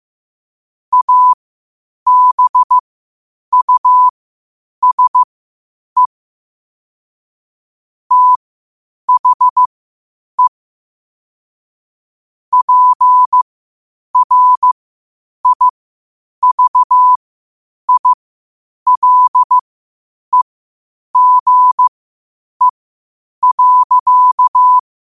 Morse code, heard as a keyed tone, about 15 wpm, spaced as for 8 wpm, 1000 Hz. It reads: ABUSE THE PRIVILEGE.
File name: qotdfiles / 8wpm